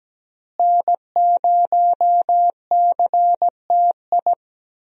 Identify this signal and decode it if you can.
Morse code, heard as a keyed tone, about 17 wpm, 700 Hz. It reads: N0CTI
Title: USA Call Signs - 5 Characters - Repeated 3x - 17wpm